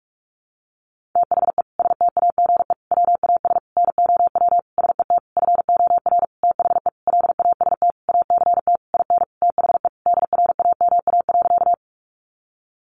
Morse code, heard as a keyed tone, about 44 wpm, 700 Hz. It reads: THE STAGE WAS NOW SET FOR THE LAST ACT IN THE DRAMA.
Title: Sentences from Top 1500 Words 45wpm